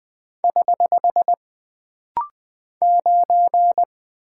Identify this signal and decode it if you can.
Morse code, heard as a keyed tone, about 20 wpm, 700 Hz. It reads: <HH> 9
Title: Single Character (Letter:Number:Punctuation:Prosign) - Rapid-Fire - 20wpm